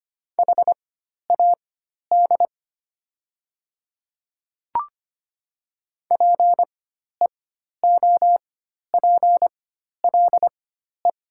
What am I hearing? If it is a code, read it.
Morse code, heard as a keyed tone, about 25 wpm, spaced as for 12 wpm, 700 Hz. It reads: HAD PEOPLE
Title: 4x Character Spacing - Top 100 Words - 25wpm